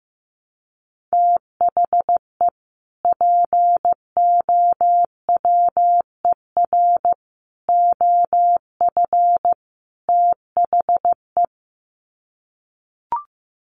Morse code, heard as a keyed tone, about 15 wpm, 700 Hz. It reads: THE POWER OF THE
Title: Sets of 4 Words 15wpm